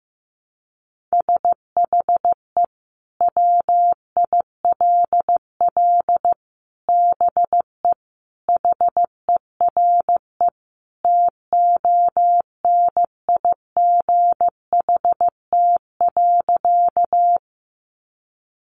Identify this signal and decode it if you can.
Morse code, heard as a keyed tone, about 15 wpm, 700 Hz. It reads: SHE WILL BE HERE TONIGHT.
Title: Sentences from Top 600 Words 15wpm